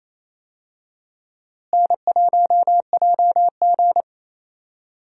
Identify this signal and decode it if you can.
Morse code, heard as a keyed tone, about 28 wpm, 700 Hz. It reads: N1JG
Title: USA Call Signs - 4 Characters - Repeated 3x - 28wpm